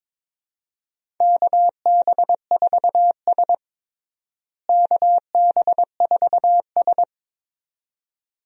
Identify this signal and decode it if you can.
Morse code, heard as a keyed tone, about 22 wpm, 700 Hz. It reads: KB4S KB4S
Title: USA Call Signs - 4 Characters - Repeated 2x - 22wpm